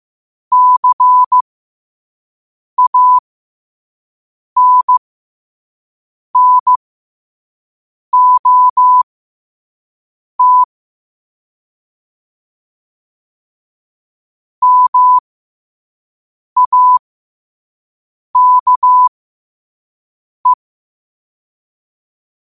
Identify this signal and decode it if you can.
Morse code, heard as a keyed tone, about 15 wpm, spaced as for 5 wpm, 1000 Hz. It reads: CANNOT MAKE
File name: qotdfiles / 5wpm